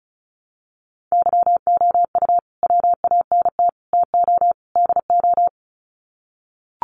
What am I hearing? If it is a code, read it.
Morse code, heard as a keyed tone, about 34 wpm, 700 Hz. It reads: YOU WANT TO DO
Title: Sets of 4 Words 35wpm